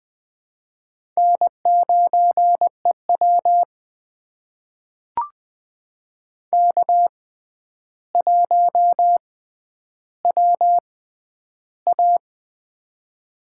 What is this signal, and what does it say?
Morse code, heard as a keyed tone, about 20 wpm, spaced as for 10 wpm, 700 Hz. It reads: N9EW K1WA
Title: USA Call Signs - 4 Characters - 6x Character Spacing - 20wpm